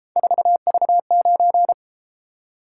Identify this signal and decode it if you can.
Morse code, heard as a keyed tone, about 32 wpm, 700 Hz. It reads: 4V9